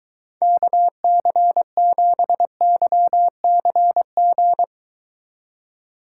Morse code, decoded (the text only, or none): KC7YCG